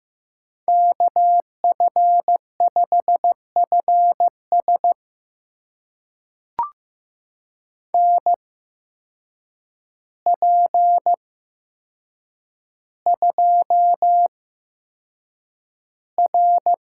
Morse code, heard as a keyed tone, about 15 wpm, spaced as for 7 wpm, 700 Hz. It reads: KF5FS NP2R
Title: USA Call Signs - 8x Character Spacing - 15wpm